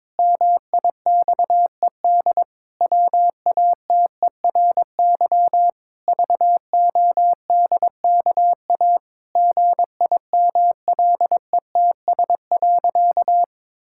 Morse code, heard as a keyed tone, about 22 wpm, 700 Hz. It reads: MIXED WATERY VODKA GIMLETS.
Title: Pangram Challenge v2 - 22wpm